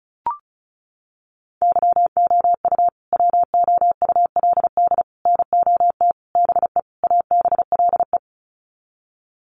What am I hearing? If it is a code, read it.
Morse code, heard as a keyed tone, about 34 wpm, 700 Hz. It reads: YOU WOULD NOT BE ABLE
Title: Sets of 5 Words Encore2 35wpm